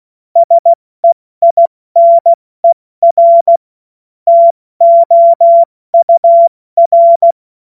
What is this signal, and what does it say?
Morse code, heard as a keyed tone, about 16 wpm, 680 Hz. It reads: SEINER TOUR
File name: Morsekurs / Witze / Witz 20. September 2022 70 F80